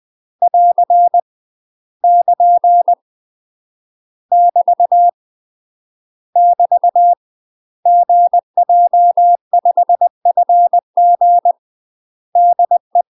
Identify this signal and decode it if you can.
Morse code, heard as a keyed tone, about 20 wpm, 700 Hz. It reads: <AR> <KN> <BT> <BT> GJ5FG DE
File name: UK MASTER QSOs TRAINING/20WPM